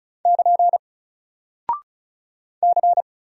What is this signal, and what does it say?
Morse code, heard as a keyed tone, about 36 wpm, 700 Hz. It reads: <KN> C
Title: Single Character (Letter:Number:Punctuation:Prosign) Alt 35wpm